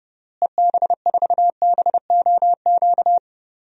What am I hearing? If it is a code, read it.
Morse code, heard as a keyed tone, about 30 wpm, 700 Hz. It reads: EB4BOQ
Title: DXCC-100 Call Signs - 6 Characters - Repeated 3x - 30wpm